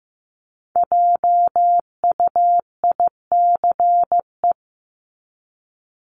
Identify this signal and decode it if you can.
Morse code, heard as a keyed tone, about 15 wpm, 700 Hz. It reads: JUICE
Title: Top 1251-1500 Words 15wpm